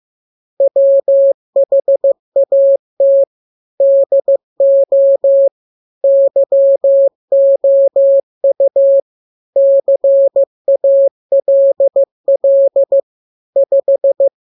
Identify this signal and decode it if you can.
Morse code, heard as a keyed tone, about 15 wpm, 550 Hz. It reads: WHAT DO YOU CALL 5
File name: jokes / joke 25 15 0000